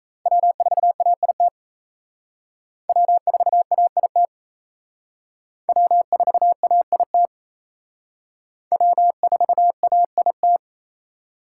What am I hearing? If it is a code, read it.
Morse code, heard as a keyed tone, about 35 wpm, 700 Hz. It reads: W4AIT W4AIT W4AIT W4AIT
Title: Speed Racing - USA 1x3 Call Signs 28wpm